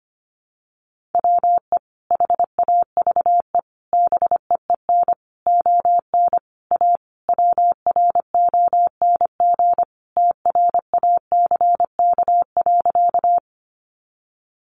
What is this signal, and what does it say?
Morse code, heard as a keyed tone, about 25 wpm, 700 Hz. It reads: WE HAVE BEEN ON A WRONG TRACK.
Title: Sentences from Top 1100 Words 25wpm